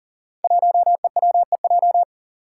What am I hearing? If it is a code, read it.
Morse code, heard as a keyed tone, about 40 wpm, 700 Hz. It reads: 1EWEJ
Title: Five Challenging Letter-Number Combinations 40wpm